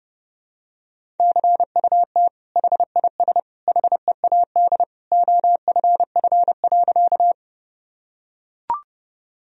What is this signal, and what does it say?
Morse code, heard as a keyed tone, about 32 wpm, 700 Hz. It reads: CUT HIS HEAD OFF.
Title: Sentences from Top 1000 Words - Review Part 1 - 30wpm